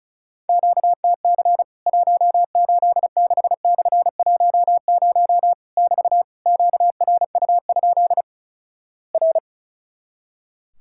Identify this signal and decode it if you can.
Morse code, heard as a keyed tone, about 36 wpm, 700 Hz. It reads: QTC 186/10 <BT> QRU? R
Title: Worked All Europe DX Contest - QTC Exchange - 35wpm